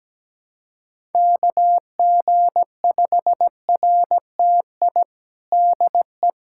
Morse code, heard as a keyed tone, about 17 wpm, 700 Hz. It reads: KG5RTI DE
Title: USA Callsign DE Callsign 17wpm